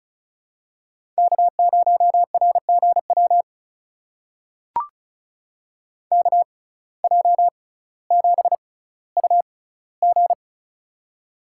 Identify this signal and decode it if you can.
Morse code, heard as a keyed tone, about 36 wpm, spaced as for 18 wpm, 700 Hz. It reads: K0RGW KJ7UG
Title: USA Call Signs - 5 Characters - 6x Character Spacing - 35wpm